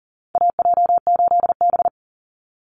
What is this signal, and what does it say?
Morse code, heard as a keyed tone, about 40 wpm, 700 Hz. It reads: AJ8B
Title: Call Signs Repeated 2x - 40wpm